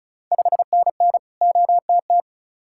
Morse code, heard as a keyed tone, about 36 wpm, 690 Hz. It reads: 5NN OTT
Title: Ontario QSO Party Exchange 35wpm